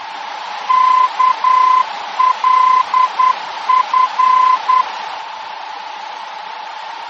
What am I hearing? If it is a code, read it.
Morse code, heard as a keyed tone, about 10 wpm, 1020 Hz, with radio noise. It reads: KLF
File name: germany / mx / KLF-D-VOR